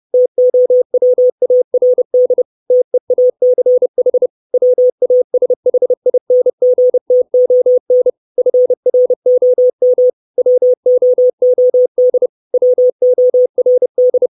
Morse code, heard as a keyed tone, about 30 wpm, 500 Hz. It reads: TOWARD TEACH WASHINGTON FROM WOOD WORD
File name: 632 Words 30WPM